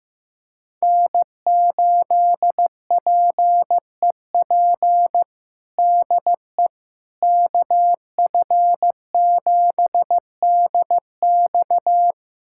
Morse code, heard as a keyed tone, about 15 wpm, 700 Hz. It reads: N8PEP DE KF7DX
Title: USA Callsign DE Callsign - 5 Characters - 15wpm